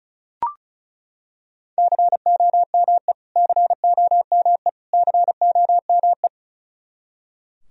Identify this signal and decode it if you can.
Morse code, heard as a keyed tone, about 36 wpm, 700 Hz. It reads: COME COME COME
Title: Top 100 Words Repeated 35wpm